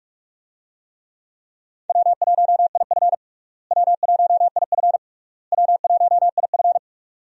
Morse code, heard as a keyed tone, about 47 wpm, 700 Hz. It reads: W1IF W1IF W1IF
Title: USA Call Signs - 4 Characters - Repeated 3x - 45wpm